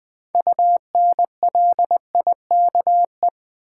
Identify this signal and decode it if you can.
Morse code, heard as a keyed tone, about 20 wpm, 700 Hz. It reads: UNLIKE
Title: Build A Word - 3 Letters At A Time - 6-Letter Words - 20wpm